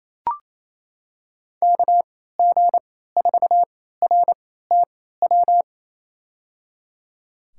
Morse code, heard as a keyed tone, about 27 wpm, spaced as for 16 wpm, 700 Hz. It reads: KG4RTW
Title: USA Call Signs - 6 Characters - 3x Character Spacing - 28wpm